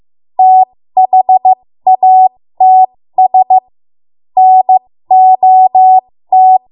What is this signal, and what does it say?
Morse code, heard as a keyed tone, about 15 wpm, spaced as for 13 wpm, 750 Hz. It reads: THATS NOT
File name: Morse / 13WPM